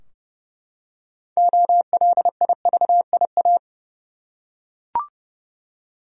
Morse code, heard as a keyed tone, about 30 wpm, 700 Hz. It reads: OLIVIA
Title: USA Common Names - 30wpm